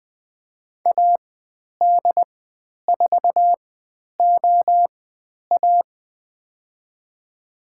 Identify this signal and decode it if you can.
Morse code, heard as a keyed tone, about 20 wpm, spaced as for 10 wpm, 700 Hz. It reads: AD4OA
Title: 095 Lesson 16g - Letter c - Call Signs 20wpm